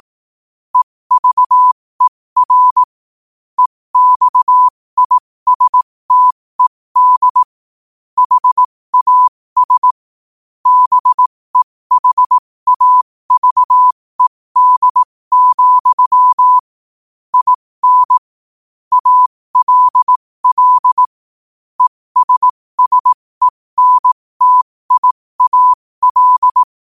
Morse code, heard as a keyed tone, about 18 wpm, spaced as for 15 wpm, 1000 Hz. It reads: EVER EXISTED HAS BEHAVED, IN ALL ESSENTIAL